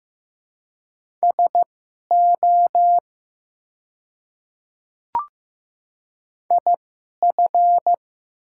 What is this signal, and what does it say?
Morse code, heard as a keyed tone, about 15 wpm, spaced as for 11 wpm, 700 Hz. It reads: SO IF
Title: 2x Character Spacing - Top 100 Words - 15wpm